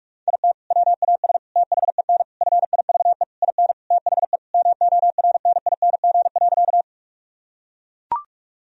Morse code, heard as a keyed tone, about 42 wpm, 700 Hz. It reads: IT WAS THEN FIVE IN THE MORNING.
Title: Sentences from Top 1000 Words - Review Part 2 - 45wpm